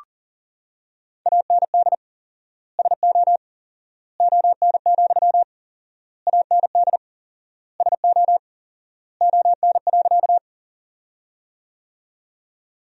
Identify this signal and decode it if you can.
Morse code, heard as a keyed tone, about 40 wpm, 700 Hz. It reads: AND SO ON, AND SO ON.
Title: 4x Word-Spacing - Sentences from Top 100 Words - 40wpm